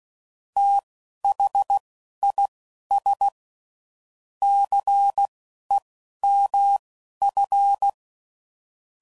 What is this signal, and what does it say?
Morse code, heard as a keyed tone, about 16 wpm, spaced as for 11 wpm, 800 Hz. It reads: THIS CEMF
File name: CW Code Groups @10WPM / Technical Subjects / XL